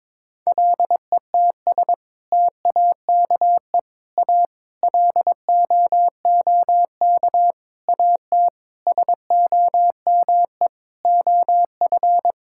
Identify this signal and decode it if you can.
Morse code, heard as a keyed tone, about 22 wpm, 700 Hz. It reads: LETS TAKE A LOOK AT SOME OF
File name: practice / 22wpm / Sets of 8 Words - 22wpm